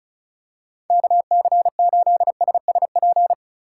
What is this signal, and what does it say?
Morse code, heard as a keyed tone, about 36 wpm, 700 Hz. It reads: KC8SSP
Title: USA Call Signs - 3x Character Spacing - 35wpm